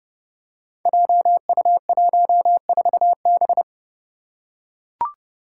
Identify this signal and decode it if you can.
Morse code, heard as a keyed tone, about 31 wpm, 700 Hz. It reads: JU14B
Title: Five Challenging Letter-Number Combinations 30wpm